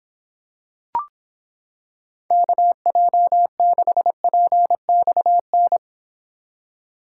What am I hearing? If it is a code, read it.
Morse code, heard as a keyed tone, about 26 wpm, 700 Hz. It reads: KJ6PXN